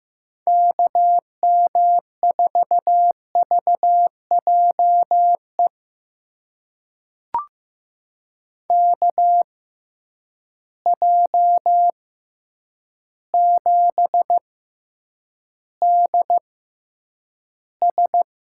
Morse code, heard as a keyed tone, about 15 wpm, spaced as for 8 wpm, 700 Hz. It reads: KM4VJE KJ7DS
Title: USA Call Signs - 6 Characters - 6x Character Spacing - 15wpm